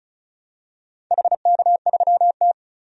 Morse code, heard as a keyed tone, about 36 wpm, 700 Hz. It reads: HK3T
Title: DXCC-100 Call Signs - 4 Characters - 35wpm